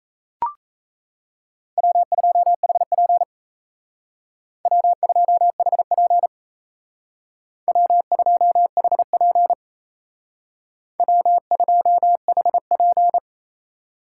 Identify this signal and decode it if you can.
Morse code, heard as a keyed tone, about 35 wpm, 700 Hz. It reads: W2HP W2HP W2HP W2HP